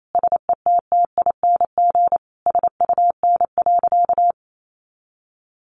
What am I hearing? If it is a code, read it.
Morse code, heard as a keyed tone, about 28 wpm, 700 Hz. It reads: SETTING SUN.
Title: Sentences from Top 1300 Words 28wpm